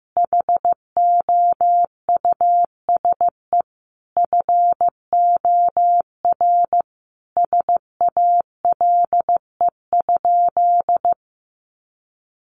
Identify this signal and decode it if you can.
Morse code, heard as a keyed tone, about 15 wpm, 700 Hz. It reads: HOUSE FOR SALE?
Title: Sentences from Top 1400 Words 15wpm